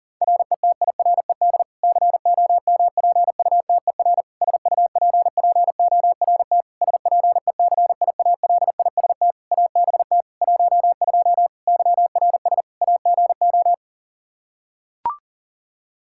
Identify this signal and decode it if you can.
Morse code, heard as a keyed tone, about 37 wpm, 700 Hz. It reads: RETIRED COMPUTER SUPPORT SPECIALIST ABT 12 YRS AGO
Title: Practice QSO - Exchange 3 - More Info - 40wpm